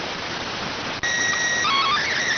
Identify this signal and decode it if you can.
SSTV (slow-scan television), PD 290.